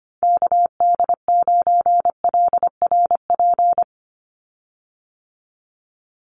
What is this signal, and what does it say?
Morse code, heard as a keyed tone, about 25 wpm, 690 Hz. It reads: KD9LRP
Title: USA Call Signs 25wpm